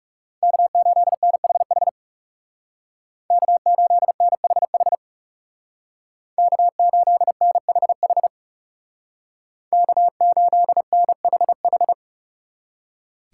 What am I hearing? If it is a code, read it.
Morse code, heard as a keyed tone, about 36 wpm, 700 Hz. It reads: K8NHH K8NHH K8NHH K8NHH
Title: Speed Racing - USA 1x3 Call Signs 30wpm